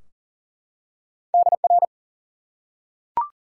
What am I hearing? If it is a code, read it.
Morse code, heard as a keyed tone, about 40 wpm, 700 Hz. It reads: DR
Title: Top 90 QSO Elements 40wpm